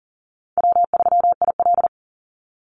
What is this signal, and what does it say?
Morse code, heard as a keyed tone, about 40 wpm, 700 Hz. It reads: W3IL
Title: Easy Call Signs Repeated 3x - 40wpm